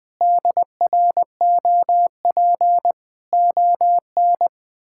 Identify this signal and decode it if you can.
Morse code, heard as a keyed tone, about 19 wpm, 700 Hz. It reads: DROP ON
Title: Pangram Challenge v2 - 20wpm